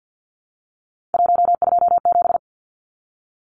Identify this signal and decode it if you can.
Morse code, heard as a keyed tone, about 50 wpm, 700 Hz. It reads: 127
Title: Numbers - 3-Digit 50wpm